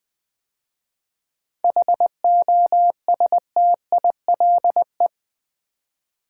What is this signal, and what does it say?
Morse code, heard as a keyed tone, about 20 wpm, 700 Hz. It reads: HOSTILE
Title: 7 Letter Words - First 3 Letters Given - 20wpm